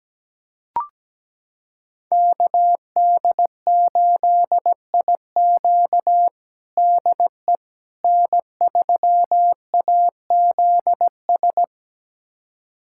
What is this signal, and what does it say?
Morse code, heard as a keyed tone, about 17 wpm, 700 Hz. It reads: KD8IQ DE N3AZS
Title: USA Callsign DE Callsign - 5 Characters - 17wpm